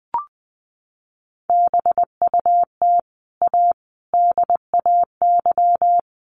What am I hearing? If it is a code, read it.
Morse code, heard as a keyed tone, about 20 wpm, 700 Hz. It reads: BUT A DAY
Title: Sentences from Top 1100 Words 20wpm